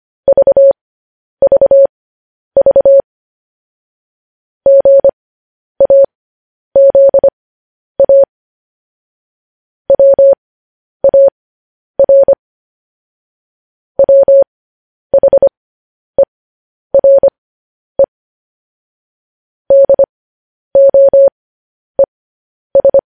Morse code, heard as a keyed tone, about 25 wpm, spaced as for 10 wpm, 550 Hz. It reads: VVV GAZA WAR WHERE DOES
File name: podcast / News headlines-1x-25x10